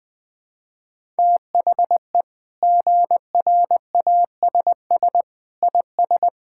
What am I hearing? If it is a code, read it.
Morse code, heard as a keyed tone, about 20 wpm, 700 Hz. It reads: THE GRASS IS
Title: Sentences with 4-Word Phrases - Spoken Beginning - 20wpm